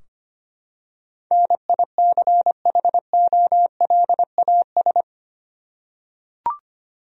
Morse code, heard as a keyed tone, about 25 wpm, 700 Hz. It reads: NICHOLAS
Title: USA Common Names - 25wpm